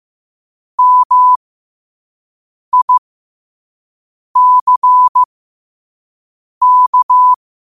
Morse code, heard as a keyed tone, about 15 wpm, spaced as for 5 wpm, 1000 Hz. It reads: MICK
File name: qotdfiles / 5wpm